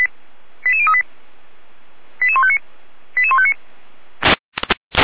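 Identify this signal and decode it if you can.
Five-tone selective call (ZVEI), heard as quick runs of five tones, tones of about 70 ms. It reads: 88038 88159 88159
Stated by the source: ZVEI five-tone selective call received on an SDR.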